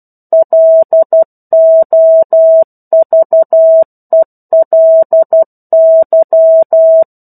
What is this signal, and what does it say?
Morse code, heard as a keyed tone, about 12 wpm, 650 Hz. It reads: LOVELY
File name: 01-Chapter-01 12 WPM